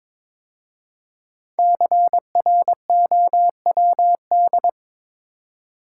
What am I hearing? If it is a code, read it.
Morse code, heard as a keyed tone, about 22 wpm, 700 Hz. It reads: CROWD
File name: 5 Letter Words - First 3 Letters Given - 22wpm